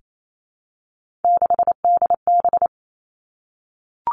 Morse code, heard as a keyed tone, about 28 wpm, 700 Hz. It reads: 6DB
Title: Three Challenging Letter-Number Combinations 28wpm